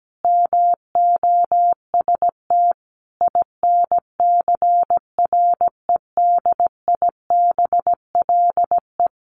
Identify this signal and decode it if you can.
Morse code, heard as a keyed tone, about 17 wpm, 700 Hz. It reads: MOST INCREDIBLE
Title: A Sound Of Thunder 17wpm